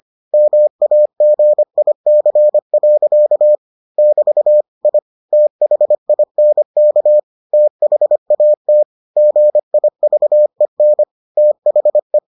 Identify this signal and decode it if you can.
Morse code, heard as a keyed tone, about 25 wpm, 600 Hz. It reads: MAGIC. <BT> I THINK THAT GIVEN THE